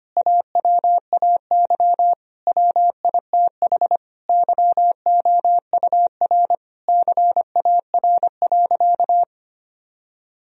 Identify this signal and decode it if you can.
Morse code, heard as a keyed tone, about 25 wpm, 700 Hz. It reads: AWAY WITH YOUR CAR.
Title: Sentences from Top 1000 Words - Review Part 1 - 25wpm